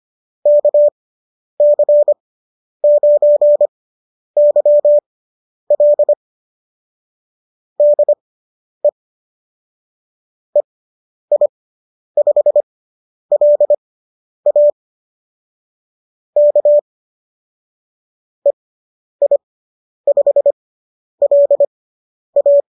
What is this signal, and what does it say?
Morse code, heard as a keyed tone, about 25 wpm, spaced as for 10 wpm, 590 Hz. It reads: KC9YL DE EI5LA K EI5LA